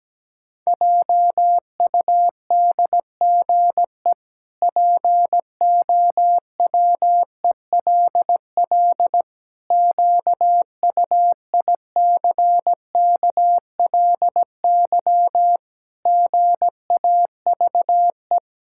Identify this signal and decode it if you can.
Morse code, heard as a keyed tone, about 17 wpm, 700 Hz. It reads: JUDGE POWELL QUICKLY GAVE
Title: Pangram Challenge v2 - 17wpm